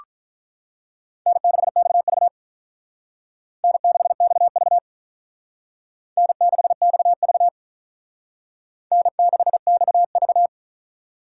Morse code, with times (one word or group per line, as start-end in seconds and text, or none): 1.3-2.3 N6XV
3.6-4.8 N6XV
6.2-7.5 N6XV
8.9-10.5 N6XV